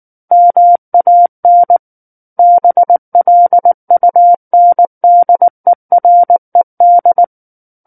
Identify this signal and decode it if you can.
Morse code, heard as a keyed tone, about 19 wpm, 700 Hz. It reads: MAN BLUNDERED
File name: Chapter 10 - 19 WPM